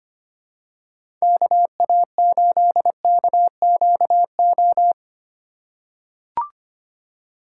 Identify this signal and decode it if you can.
Morse code, heard as a keyed tone, about 25 wpm, 700 Hz. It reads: KA8KQO
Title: USA Call Signs - 3x Character Spacing - 25wpm